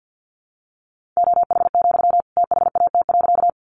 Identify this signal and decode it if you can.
Morse code, heard as a keyed tone, about 50 wpm, 700 Hz. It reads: OH, THAT.
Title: Sentences from Top 500 Words 50wpm